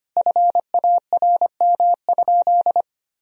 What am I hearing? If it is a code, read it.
Morse code, heard as a keyed tone, about 25 wpm, 700 Hz. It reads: FARM?
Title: Sentences from Top 1000 Words - Review Part 3 - 25wpm